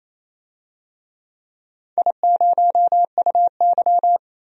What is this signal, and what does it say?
Morse code, heard as a keyed tone, about 27 wpm, 690 Hz. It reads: I0UY
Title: DXCC-100 Call Signs - 4 Characters - 28wpm